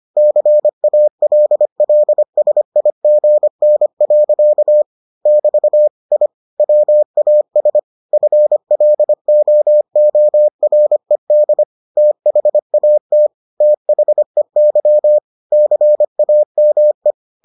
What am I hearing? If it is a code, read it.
Morse code, heard as a keyed tone, about 25 wpm, 600 Hz. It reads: CALLSIGN. <BT> I WAS FLOORED THAT THEY CAME